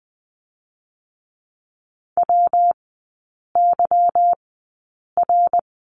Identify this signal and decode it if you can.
Morse code, heard as a keyed tone, about 20 wpm, spaced as for 8 wpm, 690 Hz. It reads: WYR
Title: 205 Lesson 33e - Letter j - Three Characters 20wpm